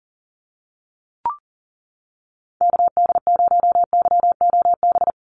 Morse code, heard as a keyed tone, about 40 wpm, 700 Hz. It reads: KD0YOB
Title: NA SOTA Activator Callsigns 40wpm